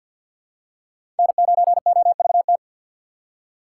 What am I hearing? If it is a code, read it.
Morse code, heard as a keyed tone, about 49 wpm, 700 Hz. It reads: N9OVT
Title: USA Call Signs - 5 Characters - 3x Character Spacing - 50wpm